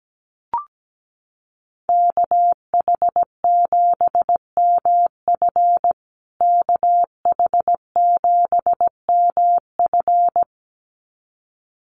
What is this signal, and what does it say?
Morse code, heard as a keyed tone, about 17 wpm, 700 Hz. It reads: KH7MF KH7MF
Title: USA Call Signs Repeated 2x 17wpm